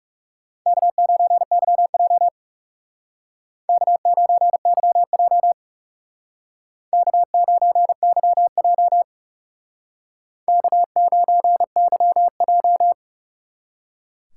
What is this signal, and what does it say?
Morse code, heard as a keyed tone, about 36 wpm, 700 Hz. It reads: K9YJ K9YJ K9YJ K9YJ
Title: Speed Racing - USA 1x2 Call Signs 30wpm